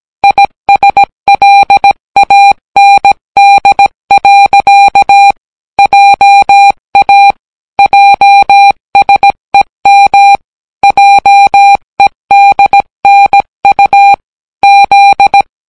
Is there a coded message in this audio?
Yes: Morse code from 0.2 to 15.4 s, heard as a keyed tone, about 17 wpm, 800 Hz. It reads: ISLAND. JA JSEM JEDNU Z